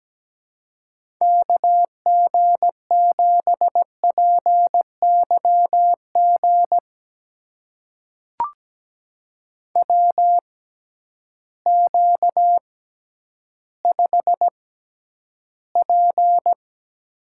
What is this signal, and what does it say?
Morse code, heard as a keyed tone, about 17 wpm, spaced as for 10 wpm, 700 Hz. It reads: KG7PYG WQ5P
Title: USA Call Signs - 6x Character Spacing - 17wpm